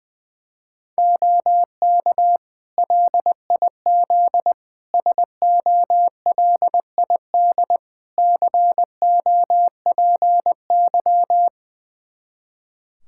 Morse code, heard as a keyed tone, about 20 wpm, 700 Hz. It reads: OK LIZ SOLID COPY
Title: K1USN SST QSO Practice 20wpm